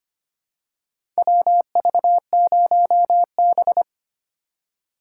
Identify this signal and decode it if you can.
Morse code, heard as a keyed tone, about 25 wpm, 700 Hz. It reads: WV0B
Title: USA Call Signs - 4 Characters - 2x Character Spacing - 25wpm